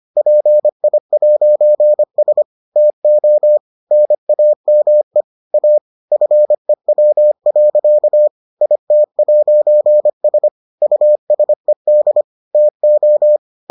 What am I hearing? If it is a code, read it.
Morse code, heard as a keyed tone, about 25 wpm, 600 Hz. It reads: PI'S TO NAME A FEW. IT'S USED TO